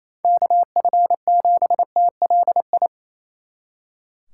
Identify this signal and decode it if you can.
Morse code, heard as a keyed tone, about 33 wpm, 700 Hz. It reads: KF7TLI
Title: Speed Racing - USA 2x3 Call Signs 28wpm